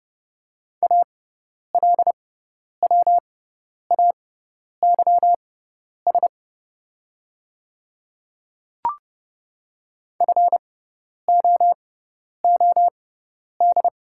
Morse code, heard as a keyed tone, about 31 wpm, spaced as for 10 wpm, 700 Hz. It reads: ALWAYS FOOD
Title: 6x Character Spacing - Top 300 Words - 30wpm